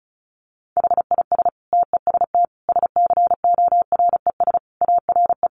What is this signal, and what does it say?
Morse code, heard as a keyed tone, about 34 wpm, 700 Hz. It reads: HIS TEST SCORES ARE
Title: Additional Easy Phrases 35wpm